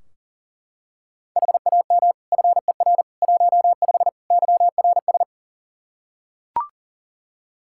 Morse code, heard as a keyed tone, about 37 wpm, 700 Hz. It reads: HAM FER 15 YRS
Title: Practice QSO - Exchange 3 - More Info - 40wpm